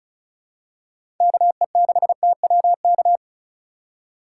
Morse code, heard as a keyed tone, about 36 wpm, 700 Hz. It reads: KE6TWK
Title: USA Call Signs - 6 Characters - 3x Character Spacing - 35wpm